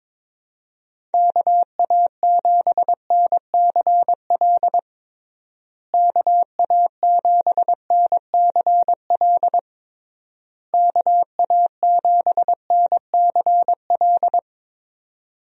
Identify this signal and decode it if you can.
Morse code, heard as a keyed tone, about 22 wpm, 700 Hz. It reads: KA7NCL KA7NCL KA7NCL
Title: USA Call Signs - 6 Characters - Repeated 3x - 22wpm